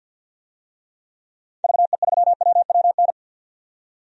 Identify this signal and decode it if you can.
Morse code, heard as a keyed tone, about 48 wpm, 700 Hz. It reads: VE3WWN
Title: DXCC-100 Call Signs - 6 Characters - 50wpm